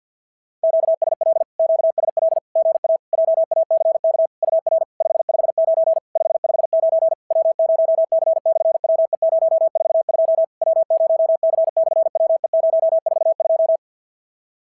Morse code, heard as a keyed tone, about 50 wpm, 650 Hz. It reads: QSL QSL GA JACK UR 559 559 W0C/WE042 W0C/WE042
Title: SOTA QSO Practice 50wpm